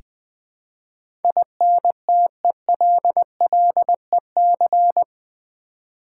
Morse code, heard as a keyed tone, about 20 wpm, 700 Hz. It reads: INTELLEC
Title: Build A Word - 4 Letters At A Time - 12-Letter Words - 20wpm